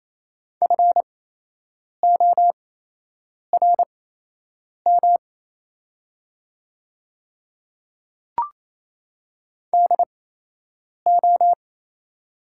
Morse code, heard as a keyed tone, about 28 wpm, spaced as for 8 wpm, 700 Hz. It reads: FORM DO
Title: 8x Character Spacing - Top 200 Words - 28wpm